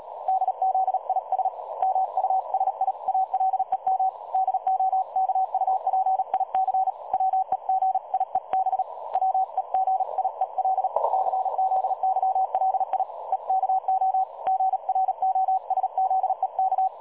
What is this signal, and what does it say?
Morse code, heard as a keyed tone, about 35 wpm, 760 Hz. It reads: DZIS MUSIALEM DOKUPIC WEGIEB WEGIEL CZYLI EKO GROSZEK
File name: dlazaawansowanych / nagraniazpasma / websdr recording 3565.0kHz